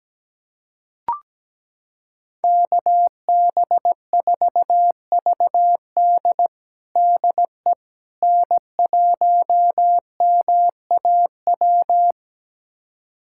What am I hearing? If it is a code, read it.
Morse code, heard as a keyed tone, about 17 wpm, 700 Hz. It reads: KB4VD DE N1MAW
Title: USA Callsign DE Callsign - 5 Characters - 17wpm